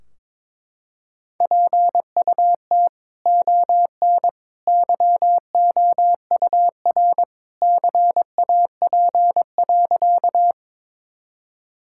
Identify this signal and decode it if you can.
Morse code, heard as a keyed tone, about 22 wpm, 700 Hz. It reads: PUT ON YOUR CAP.
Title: Sentences from Top 1600 Words 22wpm